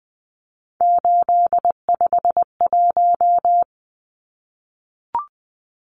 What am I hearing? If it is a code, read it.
Morse code, heard as a keyed tone, about 20 wpm, 700 Hz. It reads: 851
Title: Numbers - 3-Digit 20wpm